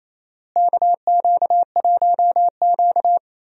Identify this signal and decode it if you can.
Morse code, heard as a keyed tone, about 27 wpm, 700 Hz. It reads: KQ1Q
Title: USA Call Signs - 4 Characters - Repeated 3x - 28wpm